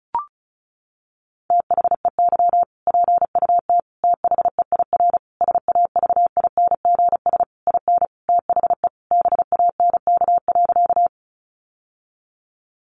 Morse code, heard as a keyed tone, about 34 wpm, 700 Hz. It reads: THEY PUT THEIR SAVINGS IN THE BANK.
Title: Sentences from Top 1200 Words 35wpm